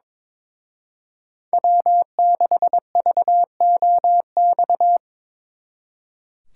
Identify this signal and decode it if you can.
Morse code, heard as a keyed tone, about 26 wpm, 700 Hz. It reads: W6VOX
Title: Speed Racing - USA 1x3 Call Signs 22wpm